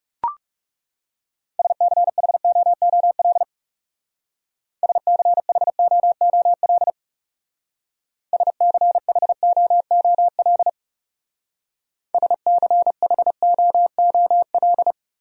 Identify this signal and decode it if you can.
Morse code, heard as a keyed tone, about 37 wpm, 700 Hz. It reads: SCHOOL SCHOOL SCHOOL SCHOOL
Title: Speed-Racing Top 500 Words 30wpm